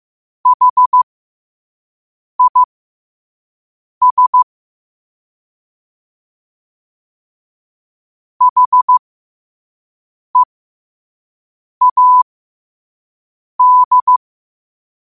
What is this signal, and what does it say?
Morse code, heard as a keyed tone, about 15 wpm, spaced as for 5 wpm, 1000 Hz. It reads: HIS HEAD
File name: qotdfiles / 5wpm